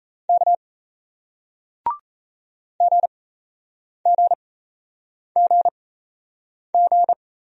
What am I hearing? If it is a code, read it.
Morse code, heard as a keyed tone, about 38 wpm, spaced as for 8 wpm, 700 Hz. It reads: K GGGG